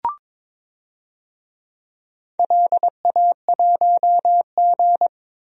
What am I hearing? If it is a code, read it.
Morse code, heard as a keyed tone, about 22 wpm, 700 Hz. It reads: LA1G
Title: DXCC-100 Call Signs - 4 Characters - 22wpm